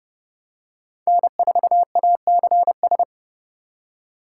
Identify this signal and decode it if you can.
Morse code, heard as a keyed tone, about 30 wpm, 700 Hz. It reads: N4ACS
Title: USA Call Signs - 5 Characters - 3x Character Spacing - 30wpm